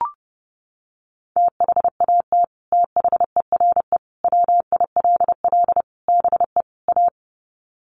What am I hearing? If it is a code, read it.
Morse code, heard as a keyed tone, about 30 wpm, 700 Hz. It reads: THAT THERE WILL BE A
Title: Sets of 5 Words Encore2 30wpm